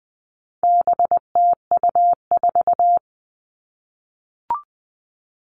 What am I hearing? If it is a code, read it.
Morse code, heard as a keyed tone, about 20 wpm, 700 Hz. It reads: BTU4
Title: All Cut Numbers - 4-Digit 20wpm